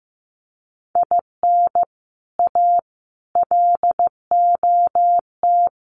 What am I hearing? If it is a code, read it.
Morse code, heard as a keyed tone, about 15 wpm, 700 Hz. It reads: IN A LOT
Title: Sets of 4 Words 15wpm